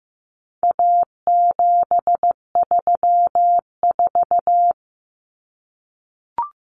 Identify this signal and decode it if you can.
Morse code, heard as a keyed tone, about 15 wpm, 700 Hz. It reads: A734
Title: Common Cut Numbers - 4-Digit 15wpm